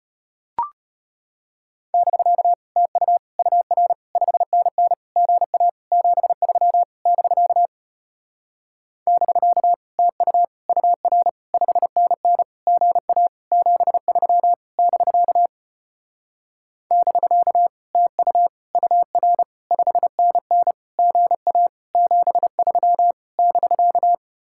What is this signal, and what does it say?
Morse code, heard as a keyed tone, about 33 wpm, 700 Hz. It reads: <BK> TU UR 5NN GA 73 <BK> <BK> TU UR 5NN GA 73 <BK> <BK> TU UR 5NN GA 73 <BK>